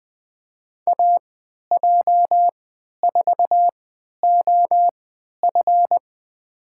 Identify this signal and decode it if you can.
Morse code, heard as a keyed tone, about 20 wpm, spaced as for 11 wpm, 700 Hz. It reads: AJ4OF
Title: USA Call Signs - 5 Characters - 3x Character Spacing - 20wpm